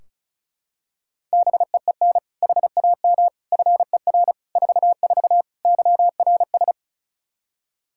Morse code, heard as a keyed tone, about 36 wpm, 700 Hz. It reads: BEEN HAM FER 44 YRS
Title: Practice QSO - Exchange 3 - More Info - 35wpm